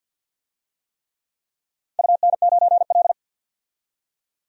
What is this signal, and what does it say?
Morse code, heard as a keyed tone, about 46 wpm, 700 Hz. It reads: UN9L
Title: DXCC-100 Call Signs - 4 Characters - 50wpm